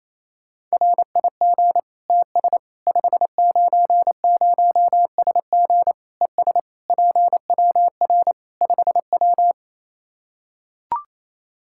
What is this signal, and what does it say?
Morse code, heard as a keyed tone, about 28 wpm, 700 Hz. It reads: RIG TS 590SG ES PWR 5W
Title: Practice QSO - Exchange 2 - Info - 28wpm